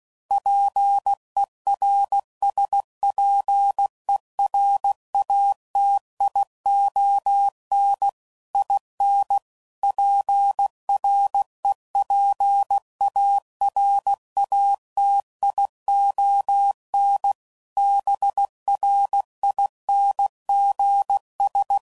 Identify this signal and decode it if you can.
Morse code, heard as a keyed tone, about 16 wpm, 800 Hz. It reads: PERSPERATION IN PREPARATION BRINGS
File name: Sayings / Quotes / Persperation